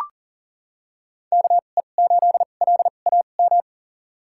Morse code, heard as a keyed tone, about 40 wpm, spaced as for 29 wpm, 700 Hz. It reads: KE8LAM
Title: USA Call Signs - 6 Characters - 2x Character Spacing - 40wpm